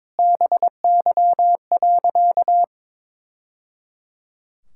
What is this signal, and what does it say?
Morse code, heard as a keyed tone, about 22 wpm, 700 Hz. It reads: BY.
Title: Sentences from Top 1000 Words - Review Part 5 - 22wpm